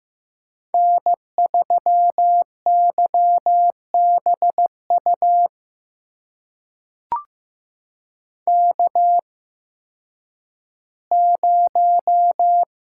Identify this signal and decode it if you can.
Morse code, heard as a keyed tone, about 15 wpm, spaced as for 10 wpm, 700 Hz. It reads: N3YBU K0
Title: USA Call Signs - 5 Characters - 8x Character Spacing - 15wpm